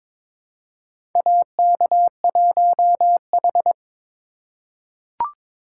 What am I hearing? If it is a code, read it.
Morse code, heard as a keyed tone, about 22 wpm, 700 Hz. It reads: AK1H
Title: USA Call Signs - 4 Characters - 2x Character Spacing - 22wpm